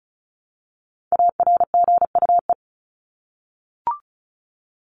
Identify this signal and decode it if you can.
Morse code, heard as a keyed tone, about 34 wpm, 700 Hz. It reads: ARGUE
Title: Top 1501-1750 Words 35wpm